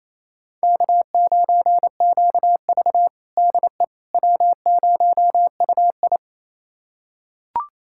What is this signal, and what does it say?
Morse code, heard as a keyed tone, about 28 wpm, 700 Hz. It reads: K9QV DE W0UI
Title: USA Callsign DE Callsign - 4 Characters - 28wpm